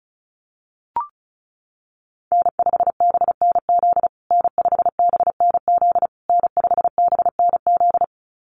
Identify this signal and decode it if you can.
Morse code, heard as a keyed tone, about 34 wpm, 700 Hz. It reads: N5BNZ N5BNZ N5BNZ
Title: USA Call Signs Repeated 3x 35wpm